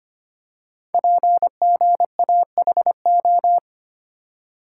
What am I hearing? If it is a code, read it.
Morse code, heard as a keyed tone, about 25 wpm, 700 Hz. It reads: PGAHO
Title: Code Groups - 5 Letters - 25wpm